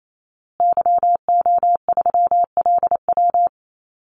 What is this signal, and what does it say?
Morse code, heard as a keyed tone, about 28 wpm, 700 Hz. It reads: YO3LW
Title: Call Signs Repeated 2x - 28wpm